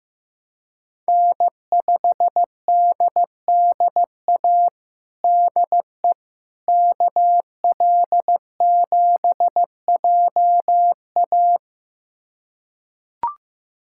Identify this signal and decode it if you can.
Morse code, heard as a keyed tone, about 16 wpm, 700 Hz. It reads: N5DDA DE KL7JA
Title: USA Callsign DE Callsign - 5 Characters - 15wpm